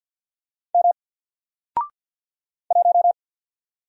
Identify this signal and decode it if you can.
Morse code, heard as a keyed tone, about 50 wpm, 700 Hz. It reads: M 1
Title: Single Letter-Number 50wpm